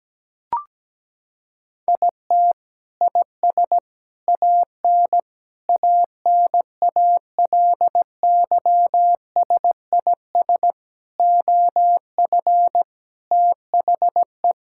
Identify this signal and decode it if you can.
Morse code, heard as a keyed tone, about 17 wpm, 700 Hz. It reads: IT IS AN ANALYSIS OF THE
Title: Sentences from Top 1000 Words - Review Part 5 - 17wpm